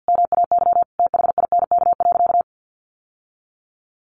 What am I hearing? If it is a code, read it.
Morse code, heard as a keyed tone, about 50 wpm, 700 Hz. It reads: MAY THINK.
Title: Sentences from Top 1100 Words 50wpm